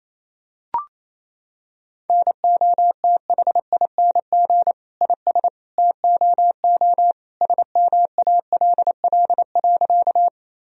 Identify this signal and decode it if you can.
Morse code, heard as a keyed tone, about 27 wpm, 700 Hz. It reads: NOTHING IS TOO SMALL.
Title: Sentences from Top 1000 Words - Review Part 1 - 28wpm